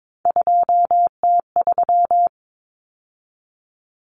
Morse code, heard as a keyed tone, about 22 wpm, 700 Hz. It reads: 2T3
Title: Common Cut Numbers - 3-Digit 22wpm